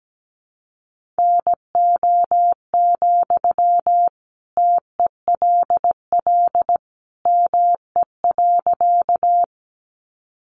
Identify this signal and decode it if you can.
Morse code, heard as a keyed tone, about 17 wpm, 700 Hz. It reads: NO, TELL ME.
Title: Sentences from Top 200 Words 17wpm